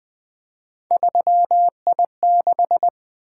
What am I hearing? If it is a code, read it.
Morse code, heard as a keyed tone, about 20 wpm, 700 Hz. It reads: 3I6
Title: Three Letter-Number Combinations - 20wpm